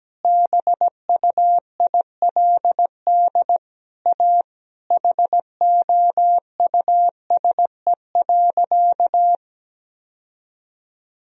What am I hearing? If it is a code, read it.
Morse code, heard as a keyed tone, about 17 wpm, 700 Hz. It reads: BUILD A HOUSE.
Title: Sentences from Top 1000 Words - Review Part 1 - 17wpm